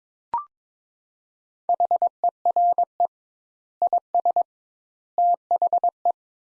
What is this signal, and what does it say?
Morse code, heard as a keyed tone, about 22 wpm, 700 Hz. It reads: HERE IS THE
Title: Sentences Top 200 Words - 22wpm